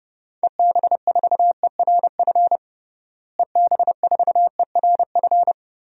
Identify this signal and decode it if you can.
Morse code, heard as a keyed tone, about 30 wpm, 700 Hz. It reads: EB4ERF EB4ERF
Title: DXCC-100 Call Signs - 6 Characters - Repeated 2x - 30wpm